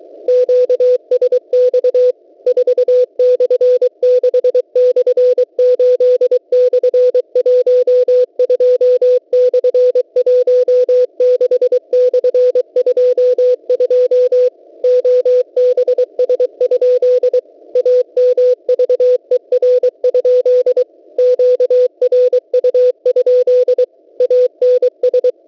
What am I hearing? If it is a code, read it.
Morse code, heard as a keyed tone, about 23 wpm, 500 Hz. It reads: QSX 4/6/8/12/16/22 OBS? AMVER? QRU? ANS
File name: WebSDR records / w7pua-2.ddns.net 8642.00 cw